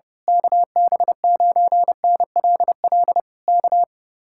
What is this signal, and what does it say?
Morse code, heard as a keyed tone, about 28 wpm, 700 Hz. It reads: KB9NLL K
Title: Speed-Racing Common QSO Phrases - 30wpm